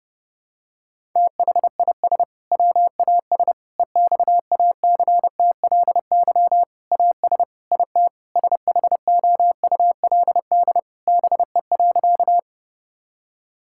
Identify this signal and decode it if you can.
Morse code, heard as a keyed tone, about 30 wpm, 700 Hz. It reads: THIS WAS EXACTLY AS IT SHOULD BE.
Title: Sentences from Top 1000 Words - Review Part 6 - 30wpm